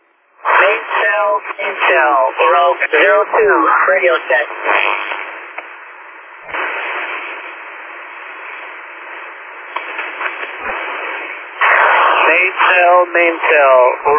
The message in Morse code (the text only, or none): none